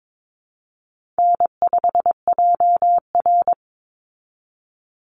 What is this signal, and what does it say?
Morse code, heard as a keyed tone, about 22 wpm, 690 Hz. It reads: N5JR
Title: Easy Call Signs Repeated 2x - 22wpm